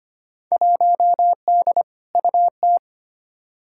Morse code, heard as a keyed tone, about 25 wpm, 700 Hz. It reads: 1D UT
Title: Field Day Exchanges - 25wpm